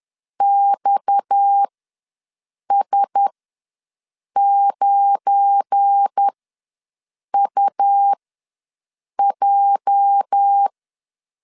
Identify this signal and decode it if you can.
Morse code, heard as a keyed tone, about 10 wpm, spaced as for 6 wpm, 800 Hz. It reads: XS9UJ
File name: LEZIONI / 8 YJ45 / 8 YJ ws 10wpm 4043